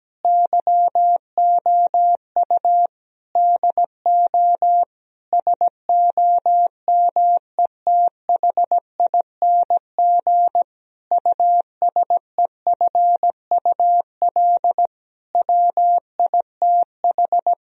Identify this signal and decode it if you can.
Morse code, heard as a keyed tone, about 17 wpm, 700 Hz. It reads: YOU DO SOMETHING USEFUL WITH
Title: Sentences from Top 1000 Words - Review Part 4 - 17wpm